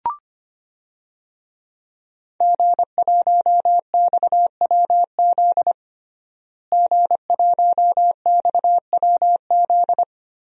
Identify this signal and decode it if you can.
Morse code, heard as a keyed tone, about 25 wpm, 700 Hz. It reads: G1XWZ G1XWZ
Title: DXCC-100 Call Signs - 5 Characters - Repeated 3x - 25wpm